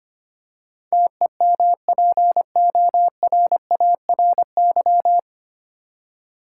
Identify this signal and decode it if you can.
Morse code, heard as a keyed tone, about 25 wpm, 700 Hz. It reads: TEMPORARY
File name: practice / 25wpm / 9 Letter Words - First 3 Letters Given - 25wpm